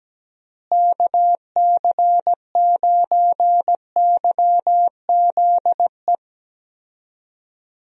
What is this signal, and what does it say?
Morse code, heard as a keyed tone, about 17 wpm, 700 Hz. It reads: KC9YZE
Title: USA Call Signs - 6 Characters - Repeated 3x - 17wpm